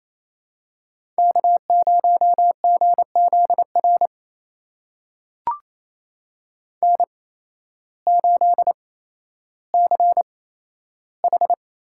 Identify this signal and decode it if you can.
Morse code, heard as a keyed tone, about 28 wpm, spaced as for 13 wpm, 700 Hz. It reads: K0GZR N8CH
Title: USA Call Signs - 8x Character Spacing - 28wpm